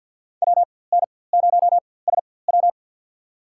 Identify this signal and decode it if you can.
Morse code, heard as a keyed tone, about 51 wpm, spaced as for 23 wpm, 700 Hz. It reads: WN0SW
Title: USA Call Signs - 5 Characters - 4x Character Spacing - 50wpm